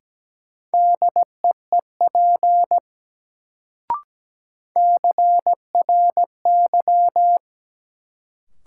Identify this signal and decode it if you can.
Morse code, heard as a keyed tone, about 17 wpm, 700 Hz. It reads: DEEP CRY
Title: Sets of 5 Words Encore - New Word Drill - 17wpm